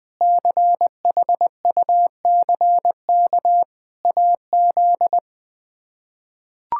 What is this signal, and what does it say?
Morse code, heard as a keyed tone, about 20 wpm, 700 Hz. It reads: CHUCK AZ
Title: K1USN SST QSO Practice - 20wpm